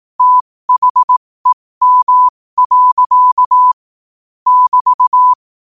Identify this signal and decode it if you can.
Morse code, heard as a keyed tone, about 18 wpm, spaced as for 15 wpm, 1000 Hz. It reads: THEM. <BT>
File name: qotdfiles / 15wpm